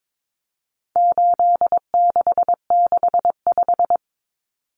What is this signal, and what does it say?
Morse code, heard as a keyed tone, about 22 wpm, 700 Hz. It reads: 8665
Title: Common Cut Numbers - 4-Digit 22wpm